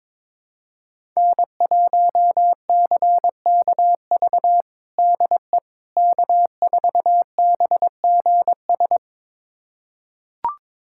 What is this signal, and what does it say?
Morse code, heard as a keyed tone, about 22 wpm, 700 Hz. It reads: N1CKV DE K4BGS
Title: USA Callsign DE Callsign - 5 Characters - 22wpm